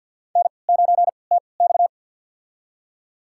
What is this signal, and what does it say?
Morse code, heard as a keyed tone, about 50 wpm, spaced as for 28 wpm, 700 Hz. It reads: N9TX
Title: USA Call Signs - 3x Character Spacing - 50wpm